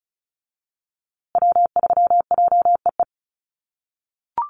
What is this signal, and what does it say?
Morse code, heard as a keyed tone, about 34 wpm, 700 Hz. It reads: W3JEE
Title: USA Call Signs Repeated 2x 35wpm